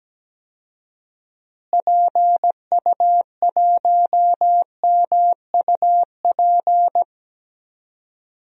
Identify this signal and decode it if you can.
Morse code, heard as a keyed tone, about 17 wpm, 700 Hz. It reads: PU1MUP
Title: DXCC-100 Call Signs - 6 Characters - Repeated 2x - 17wpm